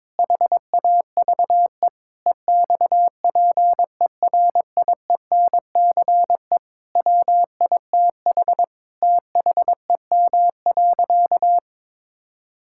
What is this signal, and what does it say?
Morse code, heard as a keyed tone, about 22 wpm, 700 Hz. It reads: HAVE EXPERIENCE WITH THEM.
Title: Sentences from Top 1000 Words - Review Part 6 - 22wpm